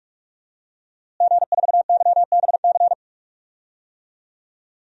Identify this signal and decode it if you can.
Morse code, heard as a keyed tone, about 42 wpm, 700 Hz. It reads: G4YBC
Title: DXCC-100 Call Signs - 5 Characters - Repeated 2x - 45wpm